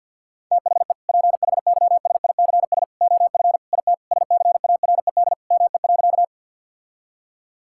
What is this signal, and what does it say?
Morse code, heard as a keyed tone, about 50 wpm, 700 Hz. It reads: THE PHYSICS OF IT SCARED ME.
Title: Sentences from Top 1000 Words - Review Part 5 - 50wpm